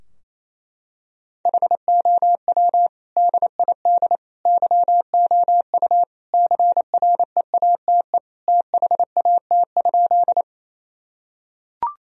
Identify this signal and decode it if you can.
Morse code, heard as a keyed tone, about 28 wpm, 700 Hz. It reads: HOW DID YOU CREATE THAT?
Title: Sentences from Top 1000 Words - Review Part 3 - 28wpm